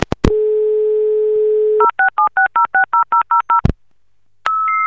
DTMF, heard as telephone touch tones, tones of about 120 ms. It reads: 0676060000